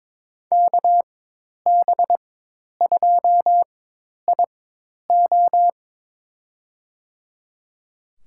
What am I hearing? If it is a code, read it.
Morse code, heard as a keyed tone, about 22 wpm, spaced as for 10 wpm, 700 Hz. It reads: KB2IO
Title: USA Call Signs - 5 Characters - 4x Character Spacing - 22wpm